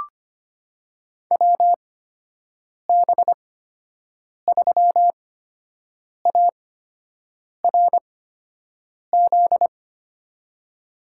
Morse code, heard as a keyed tone, about 25 wpm, spaced as for 7 wpm, 700 Hz. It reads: WB3ARZ
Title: USA Call Signs - 6 Characters - 8x Character Spacing - 25wpm